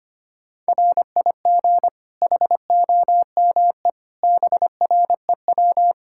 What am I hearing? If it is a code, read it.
Morse code, heard as a keyed tone, about 25 wpm, 700 Hz. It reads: RIG HOME BREW
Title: K1USN SST QSO Practice 25wpm